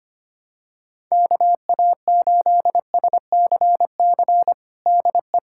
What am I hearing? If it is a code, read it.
Morse code, heard as a keyed tone, about 25 wpm, 700 Hz. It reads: KA8SCC DE
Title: USA Callsign DE Callsign 25wpm